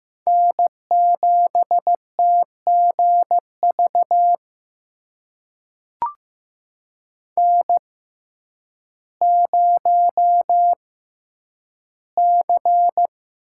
Morse code, heard as a keyed tone, about 15 wpm, spaced as for 9 wpm, 700 Hz. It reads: N7TGV N0C
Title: USA Call Signs - 5 Characters - 6x Character Spacing - 15wpm